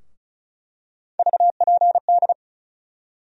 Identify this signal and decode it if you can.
Morse code, heard as a keyed tone, about 36 wpm, 700 Hz. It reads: VPD
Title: USA Call Sign Postfixes - 3 Letters 35wpm